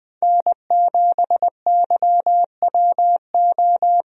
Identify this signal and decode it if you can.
Morse code, heard as a keyed tone, about 21 wpm, 700 Hz. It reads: N7YWO
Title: USA Call Signs - 5 Characters - Repeated 3x - 20wpm